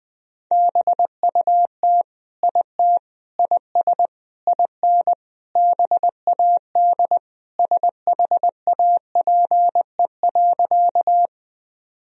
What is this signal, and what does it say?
Morse code, heard as a keyed tone, about 21 wpm, 700 Hz. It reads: BUT IT IS IN BAD SHAPE.
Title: Sentences from Top 1000 Words - Review Part 3 - 20wpm